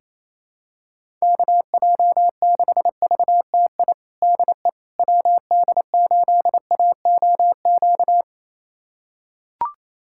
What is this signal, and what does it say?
Morse code, heard as a keyed tone, about 27 wpm, 700 Hz. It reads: KJ6VTI DE WD8AOQ
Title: USA Callsign DE Callsign - 6 Characters - 28wpm